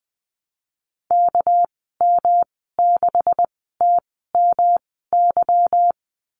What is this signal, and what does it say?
Morse code, heard as a keyed tone, about 19 wpm, spaced as for 14 wpm, 700 Hz. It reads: KM6TMY
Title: 200 Lesson 32g - Letter k - Call Signs 20wpm